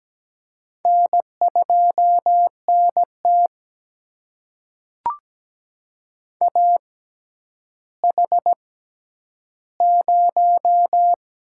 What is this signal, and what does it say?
Morse code, heard as a keyed tone, about 17 wpm, spaced as for 10 wpm, 700 Hz. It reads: N2NT AH0
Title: USA Call Signs - 4 Characters - 6x Character Spacing - 17wpm